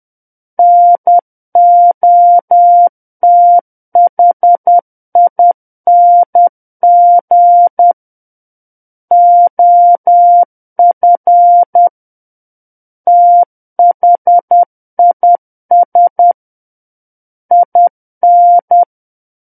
Morse code, heard as a keyed tone, about 10 wpm, 700 Hz. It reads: NOTHING OF THIS IN
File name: Book One / 01-Chapter 1 - 10 WPM